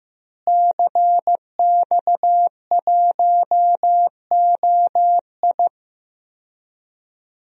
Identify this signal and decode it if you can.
Morse code, heard as a keyed tone, about 15 wpm, 700 Hz. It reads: CX1OI
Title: DXCC-100 Call Signs - 5 Characters - Repeated 2x - 15wpm